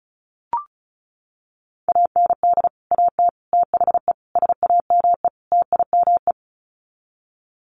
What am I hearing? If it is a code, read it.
Morse code, heard as a keyed tone, about 34 wpm, 700 Hz. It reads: AND AT THE SAME TIME
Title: Sets of 5 Words 35wpm